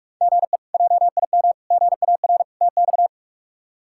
Morse code, heard as a keyed tone, about 46 wpm, 700 Hz. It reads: GE JIM GAR TX
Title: K1USN SST QSO Practice - 45wpm